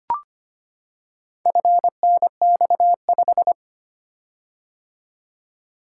Morse code, heard as a keyed tone, about 25 wpm, 700 Hz. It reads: FNX5
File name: practice / 25wpm / Four Letter-Number Combinations - 25wpm